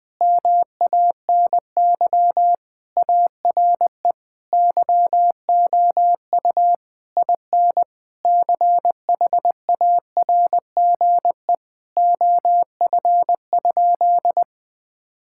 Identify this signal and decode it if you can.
Morse code, heard as a keyed tone, about 20 wpm, 700 Hz. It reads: MANY ARE YOU IN CHARGE OF?
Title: Sentences from Top 1000 Words - Review Part 6 - 20wpm